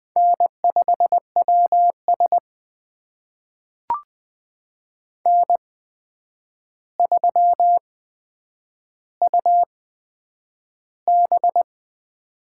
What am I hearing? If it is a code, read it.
Morse code, heard as a keyed tone, about 20 wpm, spaced as for 9 wpm, 700 Hz. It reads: N5WS N3UB
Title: USA Call Signs - 4 Characters - 8x Character Spacing - 20wpm